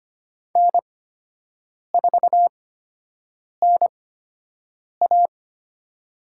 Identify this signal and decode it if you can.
Morse code, heard as a keyed tone, about 25 wpm, spaced as for 7 wpm, 700 Hz. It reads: N4NA
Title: USA Call Signs - 8x Character Spacing - 25wpm